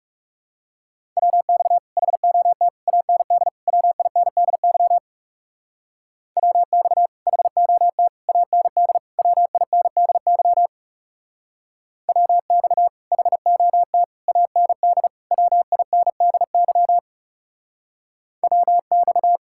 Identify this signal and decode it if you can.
Morse code, heard as a keyed tone, about 36 wpm, 700 Hz. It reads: WX HOT AND WINDY WX HOT AND WINDY WX HOT AND WINDY WX
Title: Speed-Racing Common QSO Phrases - 30wpm